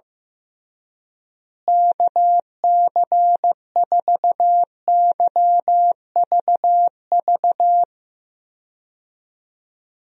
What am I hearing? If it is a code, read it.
Morse code, heard as a keyed tone, about 15 wpm, 700 Hz. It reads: KC4YVV
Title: USA Call Signs - 6 Characters - Repeated 3x - 15wpm